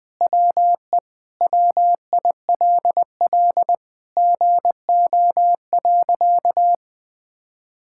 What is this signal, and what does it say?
Morse code, heard as a keyed tone, about 21 wpm, 700 Hz. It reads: WE WILL GO.
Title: Sentences with 4-Word Phrases - Spoken Beginning - 20wpm